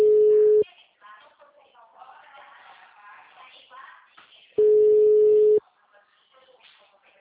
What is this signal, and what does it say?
Ringback tone (Europe, 425 Hz), with speech.